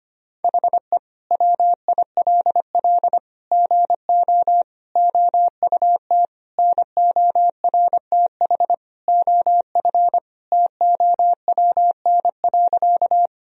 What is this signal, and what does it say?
Morse code, heard as a keyed tone, about 25 wpm, 700 Hz. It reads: HE WILL GO OUT NORTH OF TOWN.
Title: Sentences from Top 1000 Words - Review Part 1 - 25wpm